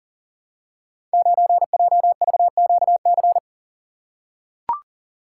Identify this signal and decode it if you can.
Morse code, heard as a keyed tone, about 37 wpm, 700 Hz. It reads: 9JVQC
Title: Code Groups - 5 Letter:Numbers - 40wpm